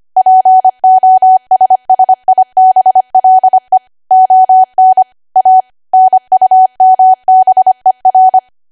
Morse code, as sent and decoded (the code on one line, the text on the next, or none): .--. --- ... ... .. -... .-.. / --- -. / .- / -. ..- -- -... .-.
POSSIBLE ON A NUMBER